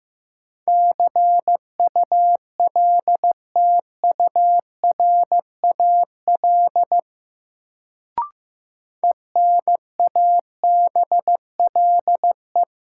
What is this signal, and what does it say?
Morse code, heard as a keyed tone, about 15 wpm, 700 Hz. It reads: CULTURAL ENABLE
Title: Sentences from Top 1500 Words - New Word Drill - 15wpm